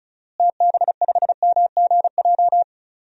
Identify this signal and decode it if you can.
Morse code, heard as a keyed tone, about 36 wpm, 700 Hz. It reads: TB5MGJ
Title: DXCC-100 Call Signs - 6 Characters - Repeated 3x - 35wpm